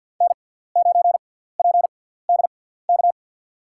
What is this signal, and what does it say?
Morse code, heard as a keyed tone, about 52 wpm, spaced as for 17 wpm, 700 Hz. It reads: N9PDK
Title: USA Call Signs - 6x Character Spacing - 50wpm